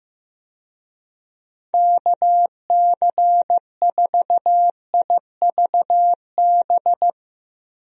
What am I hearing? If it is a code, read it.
Morse code, heard as a keyed tone, about 15 wpm, 700 Hz. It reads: KC4IVB
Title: USA Call Signs - 6 Characters - Repeated 3x - 15wpm